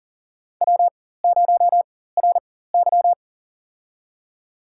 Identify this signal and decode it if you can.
Morse code, heard as a keyed tone, about 40 wpm, spaced as for 19 wpm, 700 Hz. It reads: W0RY
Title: USA Call Signs - 4 Characters - 4x Character Spacing - 40wpm